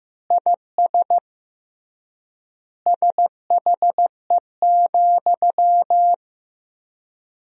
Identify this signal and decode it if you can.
Morse code, heard as a keyed tone, about 15 wpm, 700 Hz. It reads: IS SHE,
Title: 3x Word-Spacing - Sentences from Top 200 Words - 15wpm